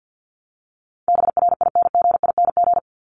Morse code, heard as a keyed tone, about 50 wpm, 700 Hz. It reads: BRINGING
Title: ING Suffix - Top 100 Words with Base Words 50wpm